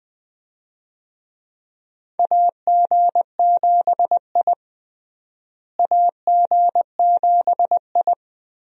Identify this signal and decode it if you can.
Morse code, heard as a keyed tone, about 20 wpm, 700 Hz. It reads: AG7I AG7I
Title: USA Call Signs - 4 Characters - Repeated 3x - 20wpm